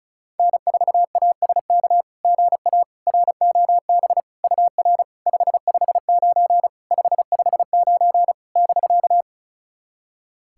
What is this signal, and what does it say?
Morse code, heard as a keyed tone, about 36 wpm, 690 Hz. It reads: N4ASK GA ROB UR 559 559 <BK>
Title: SOTA QSO Practice 35wpm